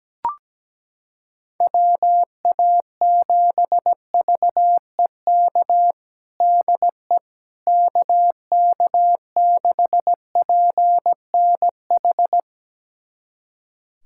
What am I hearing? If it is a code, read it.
Morse code, heard as a keyed tone, about 17 wpm, 700 Hz. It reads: WA7VEK DE KK6PNH
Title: USA Callsign DE Callsign - 6 Characters - 17wpm